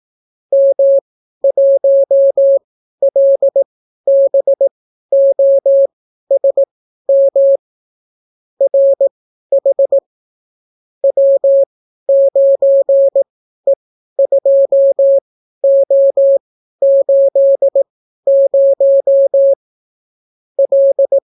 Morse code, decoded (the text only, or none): M1LBOSM RH W9E2O80 L